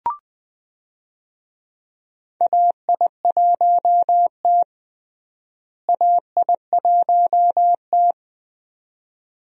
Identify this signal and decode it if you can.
Morse code, heard as a keyed tone, about 20 wpm, 700 Hz. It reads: AI1T AI1T
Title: USA Call Signs - 4 Characters - Repeated 2x - 20wpm